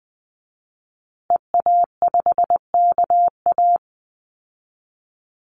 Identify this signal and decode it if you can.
Morse code, heard as a keyed tone, about 20 wpm, 700 Hz. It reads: EA5KA
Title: Call Signs Repeated 2x - 20wpm